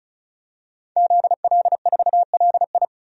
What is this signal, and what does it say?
Morse code, heard as a keyed tone, about 36 wpm, 700 Hz. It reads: ZL4LI
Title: DXCC-100 Call Signs - 5 Characters - 35wpm